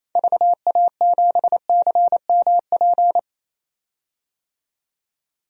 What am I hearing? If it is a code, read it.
Morse code, heard as a keyed tone, about 27 wpm, 700 Hz. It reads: VA7CMP
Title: DXCC-100 Call Signs - 6 Characters - 28wpm